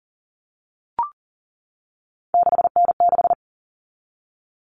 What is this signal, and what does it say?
Morse code, heard as a keyed tone, about 40 wpm, 700 Hz. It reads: 6N6
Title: All Cut Numbers - 3-Digit 40wpm